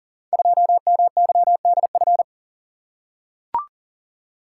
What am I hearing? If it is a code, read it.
Morse code, heard as a keyed tone, about 41 wpm, 700 Hz. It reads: 2MYDF
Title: Code Groups - 5 Letter:Numbers - 40wpm